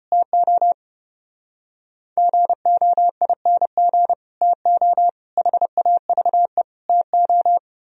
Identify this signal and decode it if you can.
Morse code, heard as a keyed tone, about 29 wpm, 700 Hz. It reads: TO GOING TO HAVE TO
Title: Speed-Racing - Sets of 4 Words 25wpm